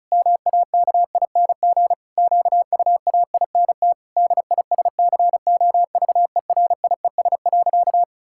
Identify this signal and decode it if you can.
Morse code, heard as a keyed tone, about 36 wpm, 700 Hz. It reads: MAKING QUAINT DISCOVERIES.
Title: Pangram Challenge v2 - 35wpm